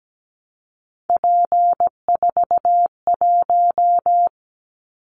Morse code, heard as a keyed tone, about 17 wpm, 700 Hz. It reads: P41
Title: Callsign Prefixes for Top 100 DXCC - 17wpm